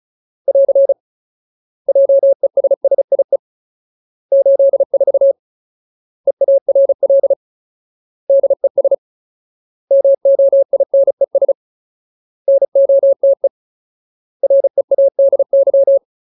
Morse code, heard as a keyed tone, about 34 wpm, 550 Hz. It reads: <AR> JESSIE 84 EARL DES MOINES NOTE READY
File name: Ultimate Word List / 35WPM Ultimate